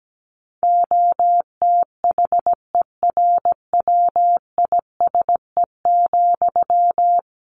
Morse code, heard as a keyed tone, about 17 wpm, 700 Hz. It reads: OTHERWISE,
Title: Sentences from Top 1400 Words 17wpm